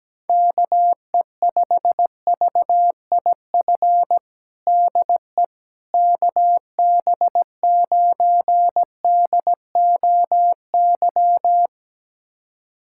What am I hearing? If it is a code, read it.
Morse code, heard as a keyed tone, about 17 wpm, 700 Hz. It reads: KE5VIF DE KB9DOY
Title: USA Callsign DE Callsign - 6 Characters - 17wpm